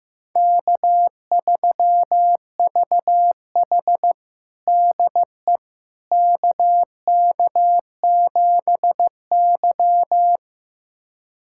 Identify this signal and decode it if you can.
Morse code, heard as a keyed tone, about 15 wpm, 700 Hz. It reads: K3VH DE KK7Y
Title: USA Callsign DE Callsign - 4 Characters - 15wpm